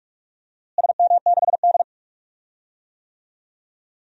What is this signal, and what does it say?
Morse code, heard as a keyed tone, about 43 wpm, 700 Hz. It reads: SM6D
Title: DXCC-100 Call Signs - 4 Characters - Repeated 2x - 45wpm